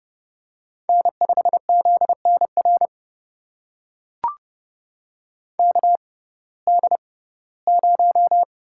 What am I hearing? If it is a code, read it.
Morse code, heard as a keyed tone, about 30 wpm, spaced as for 18 wpm, 700 Hz. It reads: N5ZNR KD0
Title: USA Call Signs - 5 Characters - 6x Character Spacing - 30wpm